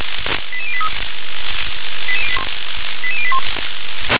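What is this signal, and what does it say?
Five-tone selective call (ZVEI), heard as quick runs of five tones, tones of about 70 ms. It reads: 88083 88081
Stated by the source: ZVEI five-tone selective call, address 88740, received on an SDR.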